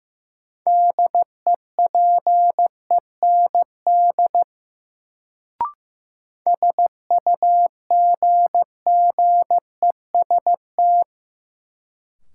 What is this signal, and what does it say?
Morse code, heard as a keyed tone, about 15 wpm, 700 Hz. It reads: DEPEND SUGGEST
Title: Sentences from Top 1300 Words - New Word Drill - 15wpm